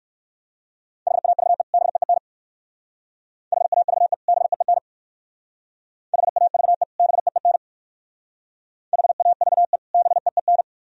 Morse code, heard as a keyed tone, about 55 wpm, 700 Hz. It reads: HAVE BEEN HAVE BEEN HAVE BEEN HAVE BEEN
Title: Speed-Racing - Sets of 2 Words 45wpm